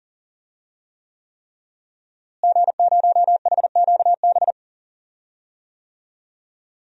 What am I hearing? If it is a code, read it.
Morse code, heard as a keyed tone, about 40 wpm, 700 Hz. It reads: G0HQB
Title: DXCC-100 Call Signs - 5 Characters - Repeated 3x - 40wpm